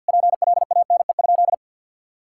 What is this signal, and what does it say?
Morse code, heard as a keyed tone, about 50 wpm, 700 Hz. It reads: PLANE?
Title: Sentences from Top 1000 Words - Review Part 6 - 50wpm